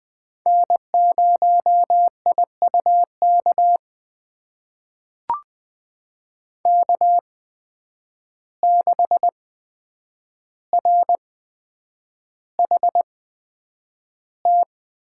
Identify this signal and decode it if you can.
Morse code, heard as a keyed tone, about 20 wpm, spaced as for 9 wpm, 700 Hz. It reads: N0IUK K6RHT
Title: USA Call Signs - 5 Characters - 8x Character Spacing - 20wpm